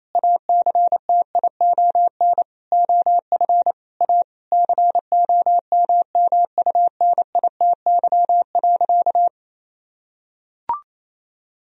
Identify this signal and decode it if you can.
Morse code, heard as a keyed tone, about 28 wpm, 700 Hz. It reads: ACTION OF A COMMUNITY.
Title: Sentences from Top 1000 Words - Review Part 4 - 28wpm